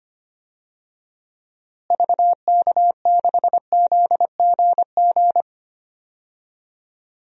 Morse code, heard as a keyed tone, about 25 wpm, 700 Hz. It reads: VK6ZGG
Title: DXCC-100 Call Signs - 6 Characters - Repeated 2x - 25wpm